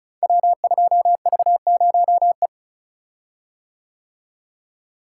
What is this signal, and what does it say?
Morse code, heard as a keyed tone, about 36 wpm, 700 Hz. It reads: W2V0E
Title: Code Groups - 5 Letter:Numbers - 35wpm